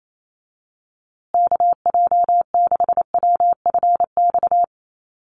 Morse code, heard as a keyed tone, about 28 wpm, 700 Hz. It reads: KJ6WFX
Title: USA Call Signs Repeated 3x 28wpm